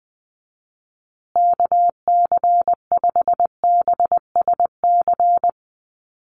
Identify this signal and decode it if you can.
Morse code, heard as a keyed tone, about 20 wpm, 700 Hz. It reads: KC5BSC
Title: USA Call Signs Repeated 3x 20wpm